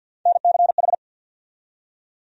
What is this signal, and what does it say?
Morse code, heard as a keyed tone, about 52 wpm, 700 Hz. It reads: NCH
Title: USA Call Sign Postfixes - 3 Letters 50wpm